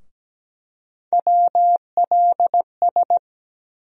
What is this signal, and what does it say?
Morse code, heard as a keyed tone, about 17 wpm, 700 Hz. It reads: WLS
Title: USA Call Sign Postfixes - 3 Letters 17wpm